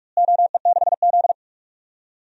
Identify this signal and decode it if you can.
Morse code, heard as a keyed tone, about 46 wpm, 700 Hz. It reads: OE6Z
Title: DXCC-100 Call Signs - 4 Characters - Repeated 3x - 45wpm